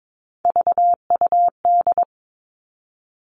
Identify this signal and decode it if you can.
Morse code, heard as a keyed tone, about 22 wpm, 700 Hz. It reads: VUD
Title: Three Challenging Letter-Number Combinations 22wpm